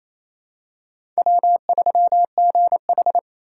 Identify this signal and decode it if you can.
Morse code, heard as a keyed tone, about 28 wpm, 700 Hz. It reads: W3GH
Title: USA Call Signs - 4 Characters - 2x Character Spacing - 28wpm